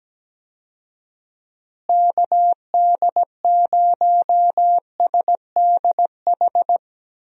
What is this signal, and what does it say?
Morse code, heard as a keyed tone, about 17 wpm, 700 Hz. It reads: KD0SDH
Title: USA Call Signs - 6 Characters - Repeated 3x - 17wpm